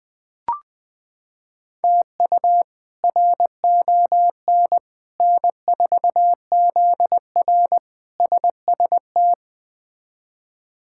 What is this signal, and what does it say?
Morse code, heard as a keyed tone, about 20 wpm, 700 Hz. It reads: TU RON N4ZR SST
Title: K1USN SST QSO Practice - 20wpm